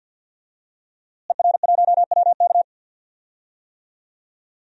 Morse code, heard as a keyed tone, about 52 wpm, 700 Hz. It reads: ER1WK
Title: DXCC-100 Call Signs - 5 Characters - Repeated 3x - 50wpm